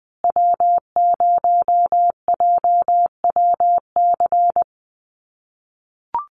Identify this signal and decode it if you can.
Morse code, heard as a keyed tone, about 20 wpm, 700 Hz. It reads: W0JWC
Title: USA Call Signs Repeated 2x 20wpm